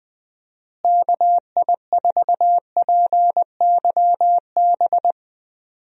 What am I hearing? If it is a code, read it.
Morse code, heard as a keyed tone, about 20 wpm, 700 Hz. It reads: KI4PYB
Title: USA Call Signs - 6 Characters - Repeated 3x - 20wpm